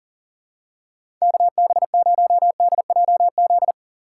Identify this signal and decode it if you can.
Morse code, heard as a keyed tone, about 40 wpm, 700 Hz. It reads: KB0DJZ
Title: USA Call Signs - 6 Characters - 3x Character Spacing - 40wpm